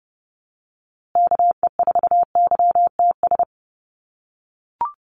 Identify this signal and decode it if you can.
Morse code, heard as a keyed tone, about 30 wpm, 700 Hz. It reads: KE4YTS
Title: USA Call Signs Repeated 3x 30wpm